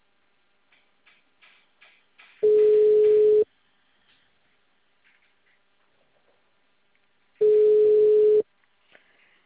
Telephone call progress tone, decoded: ringback tone

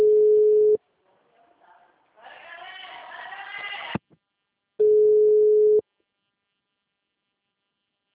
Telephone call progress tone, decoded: ringback tone

